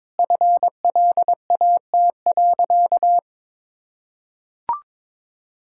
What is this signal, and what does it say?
Morse code, heard as a keyed tone, about 22 wpm, 700 Hz. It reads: FLAT.
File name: practice / 22wpm / Easy Sentences with Hints - 22wpm